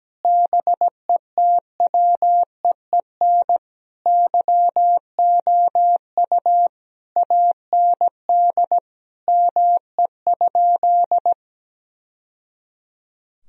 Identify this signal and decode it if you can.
Morse code, heard as a keyed tone, about 17 wpm, 700 Hz. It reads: BETWEEN YOU AND ME?
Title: Sentences from Top 1000 Words - Review Part 3 - 17wpm